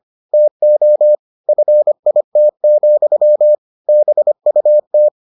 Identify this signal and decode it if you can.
Morse code, heard as a keyed tone, about 25 wpm, 600 Hz. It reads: TO FIT, BUT